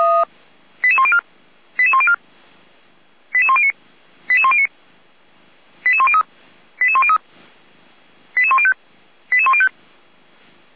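Five-tone selective call (ZVEI), heard as quick runs of five tones, tones of about 70 ms. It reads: 88104 88104 88109 88109 88103 88103 88105 88105
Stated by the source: ZVEI five-tone selective call received via SDR.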